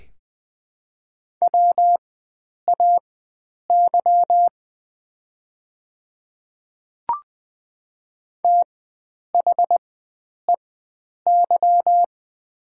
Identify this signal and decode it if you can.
Morse code, heard as a keyed tone, about 20 wpm, spaced as for 9 wpm, 700 Hz. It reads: WAY THEY
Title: 4x Character Spacing - Top 100 Words - 20wpm